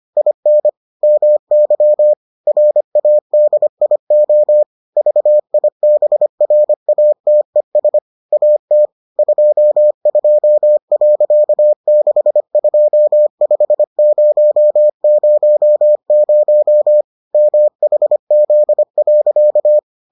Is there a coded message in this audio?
Yes: Morse code from 0.2 to 19.8 s, heard as a keyed tone, about 25 wpm, 600 Hz. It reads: IN MY RADIO VIBRATES AT 22.625000 MHZ.